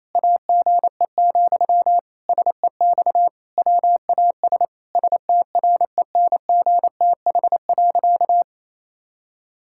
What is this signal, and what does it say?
Morse code, heard as a keyed tone, about 28 wpm, 700 Hz. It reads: AGE, SEX WAS STRENGTH.
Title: Sentences from Top 1600 Words 28wpm